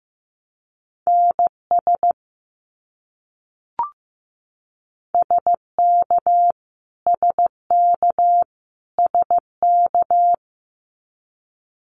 Morse code, heard as a keyed tone, about 15 wpm, 700 Hz. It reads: NS SK SK SK
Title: Canadian Provinces and Territories - Repeated 3x - 15wpm